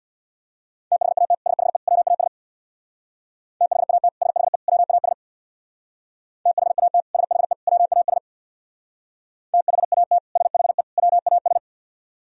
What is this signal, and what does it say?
Morse code, heard as a keyed tone, about 62 wpm, 700 Hz. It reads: THAT SHE WAS THAT SHE WAS THAT SHE WAS THAT SHE WAS
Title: Speed-Racing - Sets of 3 Words 50wpm